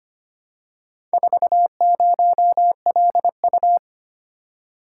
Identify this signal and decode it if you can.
Morse code, heard as a keyed tone, about 25 wpm, 700 Hz. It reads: 40LU
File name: practice / 25wpm / Four Letter-Number Combinations - 25wpm